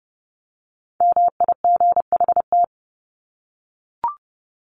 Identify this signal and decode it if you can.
Morse code, heard as a keyed tone, about 30 wpm, 700 Hz. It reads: MIGHT
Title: Top 500 Words 30wpm